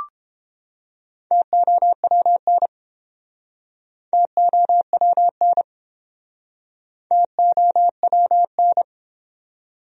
Morse code, heard as a keyed tone, about 32 wpm, 700 Hz. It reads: TOWN TOWN TOWN